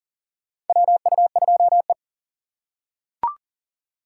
Morse code, heard as a keyed tone, about 40 wpm, 700 Hz. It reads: WU2E
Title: USA Call Signs - 4 Characters - 6x Character Spacing - 40wpm